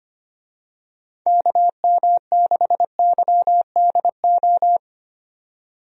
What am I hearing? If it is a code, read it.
Morse code, heard as a keyed tone, about 25 wpm, 700 Hz. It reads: KM6YDO
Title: USA Call Signs - 6 Characters - 25wpm